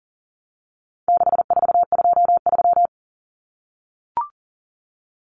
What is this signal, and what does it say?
Morse code, heard as a keyed tone, about 40 wpm, 700 Hz. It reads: 6423
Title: Numbers - 4-Digit 40wpm